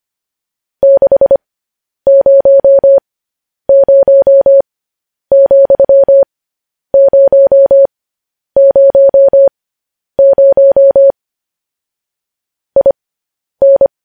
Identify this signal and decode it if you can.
Morse code, heard as a keyed tone, about 25 wpm, spaced as for 10 wpm, 550 Hz. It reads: 600,000 IN
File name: podcast / News headlines-1x-25x10